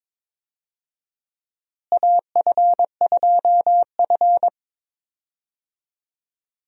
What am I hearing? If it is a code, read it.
Morse code, heard as a keyed tone, about 21 wpm, 700 Hz. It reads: AF2F
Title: USA Call Signs - 4 Characters - Repeated 3x - 22wpm